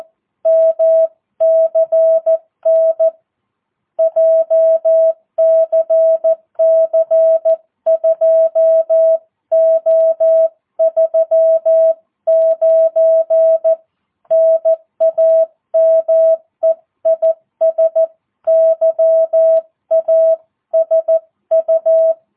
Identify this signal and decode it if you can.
Morse code, heard as a keyed tone, about 14 wpm, 650 Hz. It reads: MCN JCC 2O39 NAME IS YASU
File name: jh8cbh / 2022cw